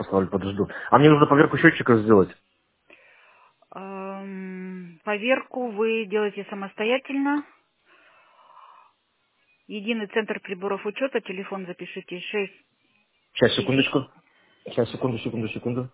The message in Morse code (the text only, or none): none